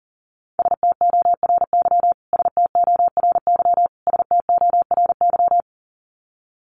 Morse code, heard as a keyed tone, about 37 wpm, 700 Hz. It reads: STORY STORY STORY
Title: Top 300 Words Repeated 40wpm